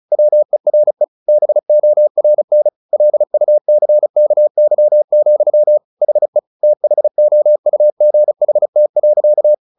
Morse code, heard as a keyed tone, about 34 wpm, 600 Hz. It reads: WERE BORN LUCKY, HE THOUGHT.
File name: the old man and the sea 2 35wpm